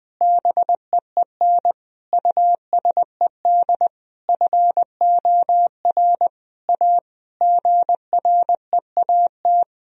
Morse code, heard as a keyed tone, about 21 wpm, 700 Hz. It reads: BEEN USED FOR A GREAT
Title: Sentences from Top 1000 Words - Review Part 1 - 20wpm